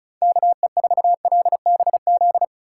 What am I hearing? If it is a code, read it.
Morse code, heard as a keyed tone, about 34 wpm, 700 Hz. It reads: KE4LBZ
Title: USA Call Signs - 6 Characters - 3x Character Spacing - 35wpm